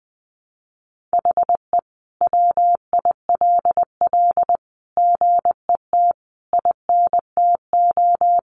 Morse code, heard as a keyed tone, about 20 wpm, 700 Hz. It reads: HE WILL GET INTO
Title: Sentences from Top 1500 Words 20wpm